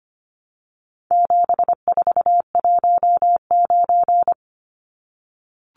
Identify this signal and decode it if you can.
Morse code, heard as a keyed tone, about 25 wpm, 700 Hz. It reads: 7419
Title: Numbers - 4-Digit 25wpm